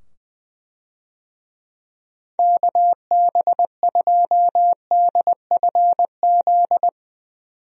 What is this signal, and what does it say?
Morse code, heard as a keyed tone, about 20 wpm, 700 Hz. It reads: KB2DFZ
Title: USA Call Signs - 6 Characters - 20wpm